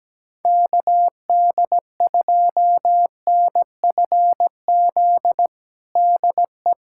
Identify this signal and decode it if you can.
Morse code, heard as a keyed tone, about 17 wpm, 700 Hz. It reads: KD2NFZ DE
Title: USA Callsign DE Callsign - 6 Characters - 17wpm